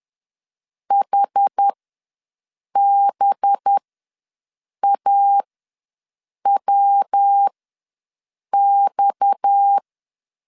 Morse code, heard as a keyed tone, about 10 wpm, spaced as for 6 wpm, 800 Hz. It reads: HBAWX
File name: LEZIONI / 6 PKX0 / 6 PKX ws 10wpm 3706